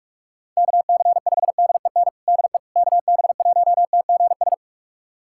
Morse code, heard as a keyed tone, about 46 wpm, 700 Hz. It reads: KC5DEN DE KB1TGS
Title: USA Callsign DE Callsign - 6 Characters - 45wpm